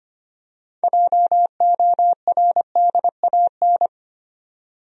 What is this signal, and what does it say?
Morse code, heard as a keyed tone, about 25 wpm, 700 Hz. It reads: JORDAN
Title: USA Common Names - 25wpm